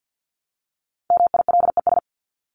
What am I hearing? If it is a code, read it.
Morse code, heard as a keyed tone, about 50 wpm, 700 Hz. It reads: MILES
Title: ES:S Suffix - Top 100 Words with Base Words 50wpm